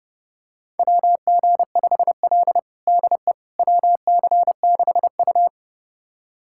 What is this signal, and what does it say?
Morse code, heard as a keyed tone, about 30 wpm, 700 Hz. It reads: WG5L DE WC6U
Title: USA Callsign DE Callsign - 4 Characters - 30wpm